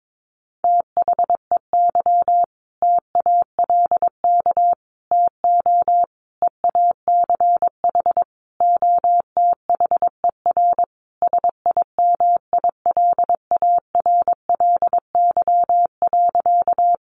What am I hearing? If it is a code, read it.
Morse code, heard as a keyed tone, about 22 wpm, 700 Hz. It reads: THEY TALK TO EACH OTHER SIMILARLY.
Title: Sentences from Top 1400 Words 22wpm